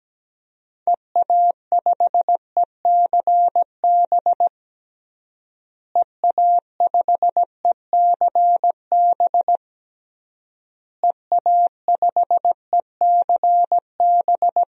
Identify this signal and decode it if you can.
Morse code, heard as a keyed tone, about 17 wpm, 700 Hz. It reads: EA5ECB EA5ECB EA5ECB
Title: DXCC-100 Call Signs - 6 Characters - Repeated 3x - 17wpm